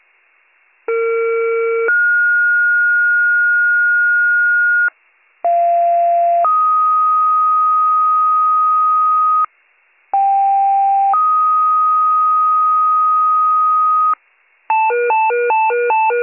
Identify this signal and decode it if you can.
Two-tone page, heard as a one-second tone, then a three-second tone, about 460 then 1430 Hz, about 690 then 1250 Hz, about 790 then 1250 Hz. It is heard from 0.9 to 14.1 s.